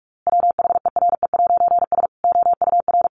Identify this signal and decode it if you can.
Morse code, heard as a keyed tone, about 44 wpm, 700 Hz. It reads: WHERE'S OUR
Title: A Sound Of Thunder 45wpm